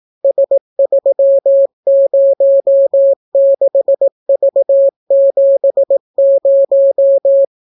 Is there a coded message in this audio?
Yes: Morse code from 0.2 to 7.5 s, heard as a keyed tone, about 17 wpm, 550 Hz. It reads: S306V70